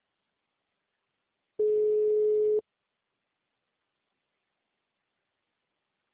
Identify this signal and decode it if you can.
Ringback tone (Europe, 425 Hz).